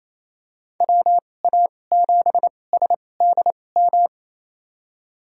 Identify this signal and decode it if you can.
Morse code, heard as a keyed tone, about 28 wpm, spaced as for 20 wpm, 700 Hz. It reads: WA7SDM
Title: USA Call Signs - 6 Characters - 2x Character Spacing - 28wpm